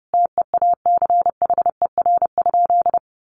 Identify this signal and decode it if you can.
Morse code, heard as a keyed tone, about 30 wpm, 700 Hz. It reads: TEACHER?
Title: Sentences from Top 1300 Words 30wpm